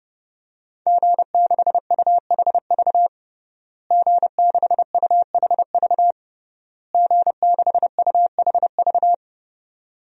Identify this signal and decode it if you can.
Morse code, heard as a keyed tone, about 29 wpm, 700 Hz. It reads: G6UHV G6UHV G6UHV
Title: DXCC-100 Call Signs - 5 Characters - Repeated 3x - 30wpm